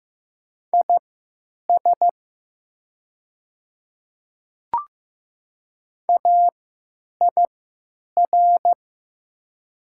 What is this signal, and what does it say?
Morse code, heard as a keyed tone, about 15 wpm, spaced as for 9 wpm, 700 Hz. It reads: IS AIR
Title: 3x Character Spacing - Top 200 Words - 15wpm